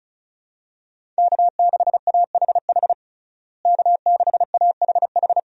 Morse code, heard as a keyed tone, about 34 wpm, 700 Hz. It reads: K6AHH K6AHH
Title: USA Call Signs - 5 Characters - Repeated 3x - 35wpm